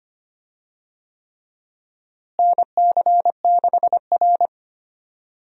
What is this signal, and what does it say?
Morse code, heard as a keyed tone, about 25 wpm, 700 Hz. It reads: NC6R
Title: USA Call Signs - 4 Characters - Repeated 3x - 25wpm